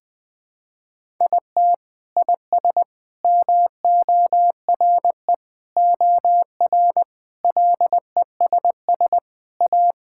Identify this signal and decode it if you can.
Morse code, heard as a keyed tone, about 20 wpm, 700 Hz. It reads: IT IS MORE OR LESS A
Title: Sentences from Top 1000 Words - Review Part 6 - 20wpm